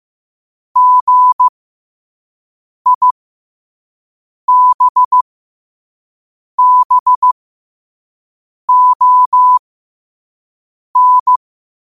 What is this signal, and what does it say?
Morse code, heard as a keyed tone, about 15 wpm, spaced as for 5 wpm, 1000 Hz. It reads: GIBBON
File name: qotdfiles / 5wpm